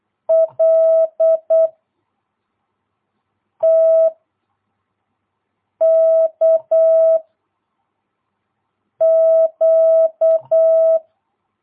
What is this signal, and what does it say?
Morse code, heard as a keyed tone, about 8 wpm, 660 Hz. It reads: L T K Q
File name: jh8cbh / 2022cw